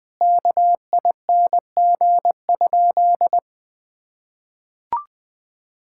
Morse code, heard as a keyed tone, about 20 wpm, 700 Hz. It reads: KING?
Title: Sentences from Top 1000 Words - Review Part 6 - 20wpm